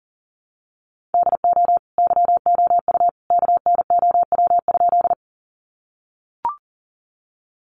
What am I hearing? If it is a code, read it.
Morse code, heard as a keyed tone, about 40 wpm, 700 Hz. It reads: DO YOU KNOW?
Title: Sentences from Top 100 Words 40wpm